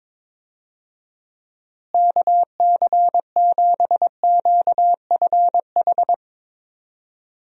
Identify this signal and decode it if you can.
Morse code, heard as a keyed tone, about 22 wpm, 700 Hz. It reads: KC7QFH
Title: USA Call Signs - 6 Characters - Repeated 3x - 22wpm